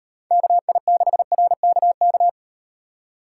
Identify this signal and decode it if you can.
Morse code, heard as a keyed tone, about 35 wpm, 700 Hz. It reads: KI6RKK